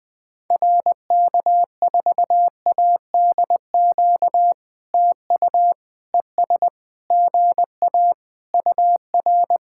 Morse code, heard as a keyed tone, about 19 wpm, 700 Hz. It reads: RK4ADQ TU ES GA UR
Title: POTA QSO Practice - 20wpm